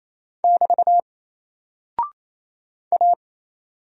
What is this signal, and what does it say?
Morse code, heard as a keyed tone, about 27 wpm, 700 Hz. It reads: <BT> A
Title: Single Character (Letter:Number:Punctuation:Prosign) 28wpm